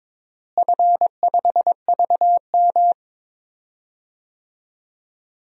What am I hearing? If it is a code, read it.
Morse code, heard as a keyed tone, about 22 wpm, 700 Hz. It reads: F5VM
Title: DXCC-100 Call Signs - 4 Characters - Repeated 2x - 22wpm